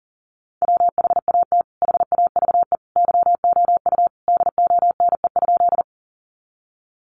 Morse code, heard as a keyed tone, about 40 wpm, 700 Hz. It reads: WHAT HAVE YOU DONE?
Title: Sentences from Top 500 Words 40wpm